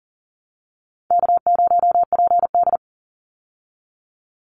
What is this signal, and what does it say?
Morse code, heard as a keyed tone, about 40 wpm, 700 Hz. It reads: K0PD
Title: Easy Call Signs Repeated 2x - 40wpm